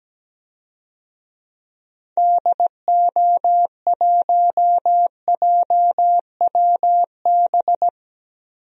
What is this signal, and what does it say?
Morse code, heard as a keyed tone, about 17 wpm, 700 Hz. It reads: DO1JWB
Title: DXCC-100 Call Signs - 6 Characters - 17wpm